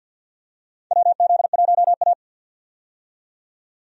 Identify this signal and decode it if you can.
Morse code, heard as a keyed tone, about 48 wpm, 700 Hz. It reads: WZ1A
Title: USA Call Signs - 4 Characters - 50wpm